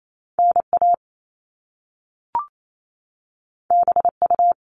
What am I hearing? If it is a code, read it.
Morse code, heard as a keyed tone, about 28 wpm, 700 Hz. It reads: NA BU
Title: All Cut Numbers - 2-Digit 28wpm